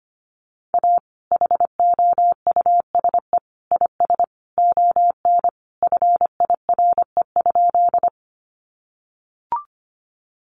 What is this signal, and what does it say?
Morse code, heard as a keyed tone, about 25 wpm, 700 Hz. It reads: A HOUSE IS ON FIRE?
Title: Sentences from Top 400 Words 25wpm